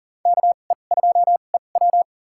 Morse code, heard as a keyed tone, about 40 wpm, spaced as for 29 wpm, 700 Hz. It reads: KE2EW
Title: USA Call Signs - 2x Character Spacing - 40wpm